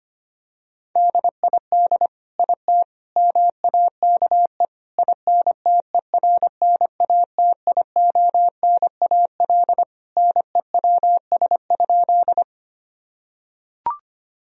Morse code, heard as a keyed tone, about 25 wpm, 700 Hz. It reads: DID IT MAKE INTERNATIONAL NEWS?
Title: Sentences from Top 1000 Words - Review Part 4 - 25wpm